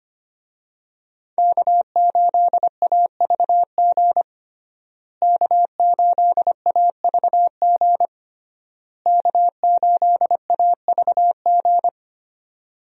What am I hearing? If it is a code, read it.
Morse code, heard as a keyed tone, about 25 wpm, 700 Hz. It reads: K8AVG K8AVG K8AVG
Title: USA Call Signs - 5 Characters - Repeated 3x - 25wpm